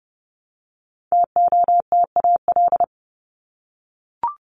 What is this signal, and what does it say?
Morse code, heard as a keyed tone, about 30 wpm, 700 Hz. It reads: TOTAL
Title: LY Suffix - Top 100 Words with Base Words 30wpm